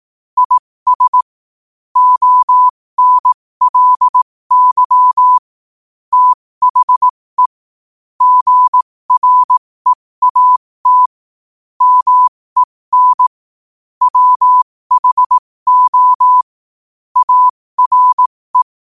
Morse code, heard as a keyed tone, about 18 wpm, spaced as for 15 wpm, 1000 Hz. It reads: IS ONLY THE GREAT MEN WHO ARE
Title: 15 WPM morse code quotes for Fri, 01 Mar 2024